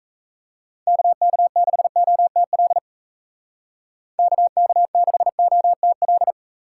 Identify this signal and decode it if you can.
Morse code, heard as a keyed tone, about 37 wpm, 700 Hz. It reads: KK6OTL KK6OTL